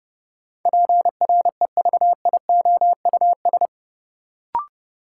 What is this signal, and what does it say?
Morse code, heard as a keyed tone, about 32 wpm, 700 Hz. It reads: PREVIOUS
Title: Sentences from Top 1500 Words - New Word Drill Alt - 30wpm